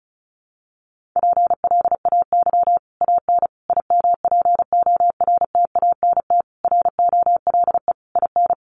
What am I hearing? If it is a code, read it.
Morse code, heard as a keyed tone, about 34 wpm, 700 Hz. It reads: PLAY AN IMPORTANT ROLE IN
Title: Sets of 5 Words Encore 35wpm